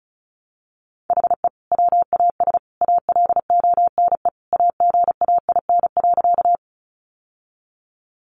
Morse code, heard as a keyed tone, about 34 wpm, 700 Hz. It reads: HE WAS ALONE AGAIN.
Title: Sentences from Top 1000 Words 35wpm